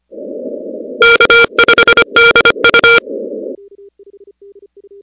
Morse code, heard as a keyed tone, about 25 wpm, 1400 Hz, with noise. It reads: K5DU
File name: contests / 2019-cw-wpx-kt5j-k5pi- -2330 0-16